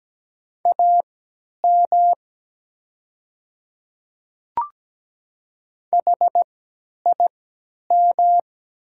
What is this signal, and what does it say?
Morse code, heard as a keyed tone, about 17 wpm, spaced as for 10 wpm, 700 Hz. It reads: AM HIM
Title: 3x Character Spacing - Top 500 Words - 17wpm